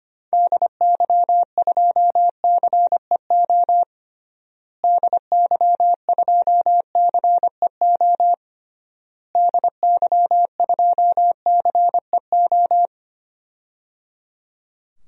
Morse code, heard as a keyed tone, about 25 wpm, 700 Hz. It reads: DY2CEO DY2CEO DY2CEO
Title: DXCC-100 Call Signs - 6 Characters - Repeated 3x - 25wpm